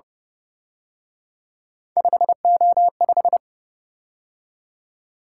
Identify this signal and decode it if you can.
Morse code, heard as a keyed tone, about 30 wpm, 700 Hz. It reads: 5O5